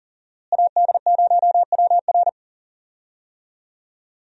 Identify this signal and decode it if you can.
Morse code, heard as a keyed tone, about 40 wpm, 700 Hz. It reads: AD0WR
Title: USA Call Signs - 5 Characters - Repeated 3x - 40wpm